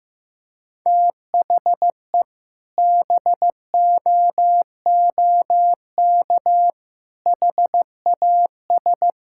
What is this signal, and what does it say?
Morse code, heard as a keyed tone, about 15 wpm, 700 Hz. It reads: THE BOOK HAS
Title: Sentences from Top 1000 Words - Review Part 6 - 15wpm